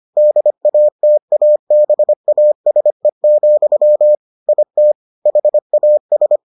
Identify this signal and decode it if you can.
Morse code, heard as a keyed tone, about 25 wpm, 600 Hz. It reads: DATABASE, IT HAS